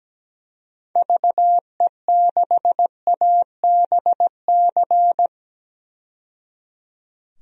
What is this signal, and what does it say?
Morse code, heard as a keyed tone, about 17 wpm, 700 Hz. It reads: VE6ABC
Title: DXCC-100 Call Signs - 6 Characters - 17wpm